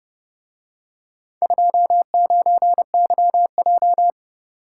Morse code, heard as a keyed tone, about 30 wpm, 700 Hz. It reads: 29YJ